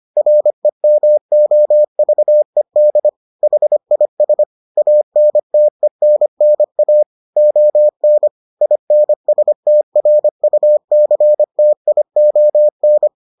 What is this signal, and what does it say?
Morse code, heard as a keyed tone, about 25 wpm, 600 Hz. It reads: REMOVED HIS ANTENNA ON INSTRUCTION